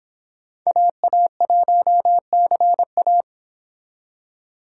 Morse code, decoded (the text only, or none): AA1CA